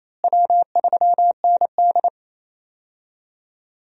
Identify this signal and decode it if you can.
Morse code, heard as a keyed tone, about 28 wpm, 700 Hz. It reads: W3ND
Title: USA Call Signs - 4 Characters - Repeated 3x - 28wpm